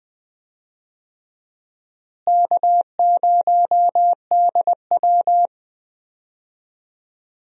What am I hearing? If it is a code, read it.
Morse code, heard as a keyed tone, about 20 wpm, 700 Hz. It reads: K0DW
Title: USA Call Signs - 4 Characters - Repeated 3x - 20wpm